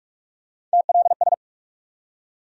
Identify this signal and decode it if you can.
Morse code, heard as a keyed tone, about 44 wpm, 700 Hz. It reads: TLS
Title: USA Call Sign Postfixes - 3 Letters 45wpm